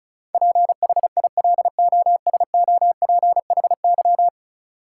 Morse code, heard as a keyed tone, about 36 wpm, 700 Hz. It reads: PHILOSOPHY
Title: Sentences from Top 900 Words - New Word Drill - 35wpm